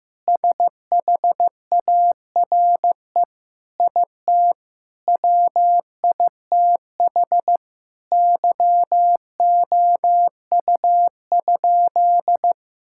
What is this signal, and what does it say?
Morse code, heard as a keyed tone, about 14 wpm, 700 Hz. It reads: SHARE IT WITH YOU?
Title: Sentences from Top 1000 Words - Review Part 6 - 15wpm